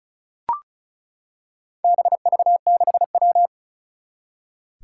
Morse code, heard as a keyed tone, about 36 wpm, 700 Hz. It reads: BV6W
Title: Four Challenging Letter-Number Combinations 35wpm